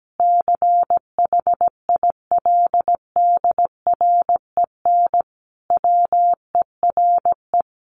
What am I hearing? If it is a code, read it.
Morse code, heard as a keyed tone, about 17 wpm, 700 Hz. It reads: CHILDREN WERE
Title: Sentences from Top 1300 Words 17wpm